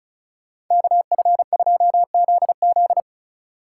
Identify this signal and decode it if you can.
Morse code, heard as a keyed tone, about 43 wpm, 700 Hz. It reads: KF2ZZ
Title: Speed Racing - USA 2x2 Call Signs 35wpm